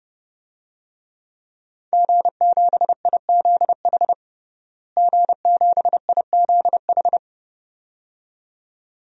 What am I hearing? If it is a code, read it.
Morse code, heard as a keyed tone, about 30 wpm, 700 Hz. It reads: G7IZH G7IZH
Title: DXCC-100 Call Signs - 5 Characters - Repeated 2x - 30wpm